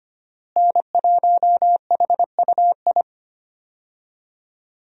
Morse code, heard as a keyed tone, about 25 wpm, 700 Hz. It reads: N1HUI
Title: USA Call Signs - 5 Characters - Repeated 3x - 25wpm